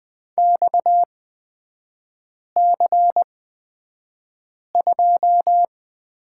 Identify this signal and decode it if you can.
Morse code, heard as a keyed tone, about 20 wpm, spaced as for 5 wpm, 690 Hz. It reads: XC2
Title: 229 Lesson 37e - Letter x - Three Characters 20wpm